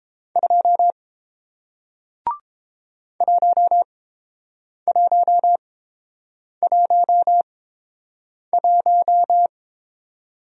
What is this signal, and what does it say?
Morse code, heard as a keyed tone, about 29 wpm, spaced as for 7 wpm, 700 Hz. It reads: 2 1111